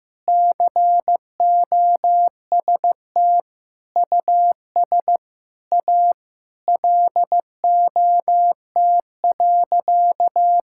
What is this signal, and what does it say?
Morse code, heard as a keyed tone, about 15 wpm, 700 Hz. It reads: COST US A LOT.
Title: Sentences from Top 1000 Words - Review Part 5 - 15wpm